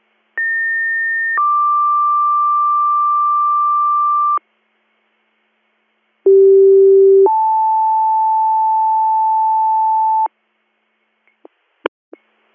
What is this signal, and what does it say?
Two-tone page, heard as a one-second tone, then a three-second tone, about 1800 then 1180 Hz, about 390 then 880 Hz. It is heard from 0.4 to 10.3 s.